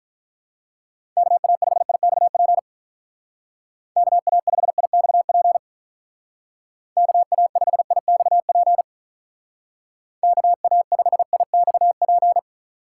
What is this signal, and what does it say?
Morse code, heard as a keyed tone, about 45 wpm, 700 Hz. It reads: KA5IXP KA5IXP KA5IXP KA5IXP